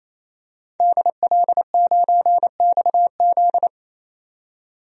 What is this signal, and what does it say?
Morse code, heard as a keyed tone, about 28 wpm, 700 Hz. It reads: DL9XZ
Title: DXCC-100 Call Signs - 5 Characters - 28wpm